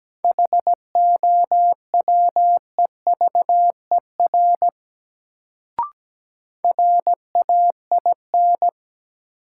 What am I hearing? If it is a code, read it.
Morse code, heard as a keyed tone, about 17 wpm, 700 Hz. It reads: HOWEVER RAIN
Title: Sentences from Top 600 Words - New Word Drill - 17wpm